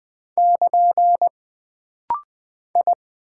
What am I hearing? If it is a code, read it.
Morse code, heard as a keyed tone, about 20 wpm, 700 Hz. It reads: <KN> I
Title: Single Character (Letter:Number:Punctuation:Prosign) - Rapid-Fire - 20wpm